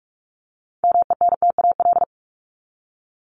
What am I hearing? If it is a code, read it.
Morse code, heard as a keyed tone, about 44 wpm, 700 Hz. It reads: MENTAL
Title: Top 501-750 Words 45wpm